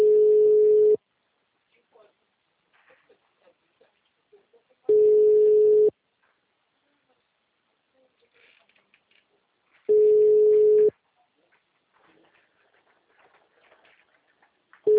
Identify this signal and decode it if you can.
Ringback tone (Europe, 425 Hz), with speech.